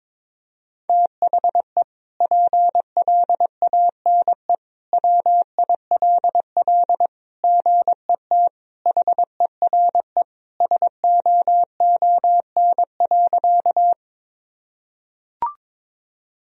Morse code, heard as a keyed tone, about 21 wpm, 700 Hz. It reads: THE PLANE WILL GET HERE SOON.
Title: Sentences from Top 1000 Words - Review Part 3 - 22wpm